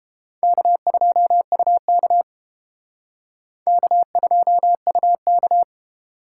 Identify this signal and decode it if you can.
Morse code, heard as a keyed tone, about 28 wpm, 700 Hz. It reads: K2UK K2UK